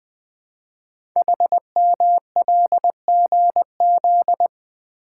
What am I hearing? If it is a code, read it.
Morse code, heard as a keyed tone, about 20 wpm, 700 Hz. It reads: HMLGZ
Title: Code Groups - 5 Letters - 20wpm